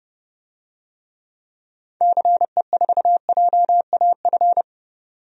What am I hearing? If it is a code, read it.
Morse code, heard as a keyed tone, about 30 wpm, 700 Hz. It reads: CE4JAF
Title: DXCC-100 Call Signs - 6 Characters - 30wpm